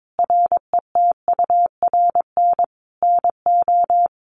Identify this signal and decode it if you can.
Morse code, heard as a keyed tone, about 22 wpm, 700 Hz. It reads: RETURN NO
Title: Sentences from Top 800 Words 22wpm